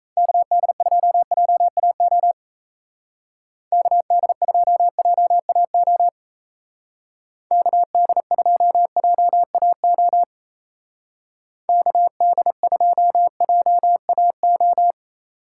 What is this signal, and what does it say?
Morse code, heard as a keyed tone, about 35 wpm, 700 Hz. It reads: KD2JAO KD2JAO KD2JAO KD2JAO
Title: Speed Racing - USA 2x3 Call Signs 28wpm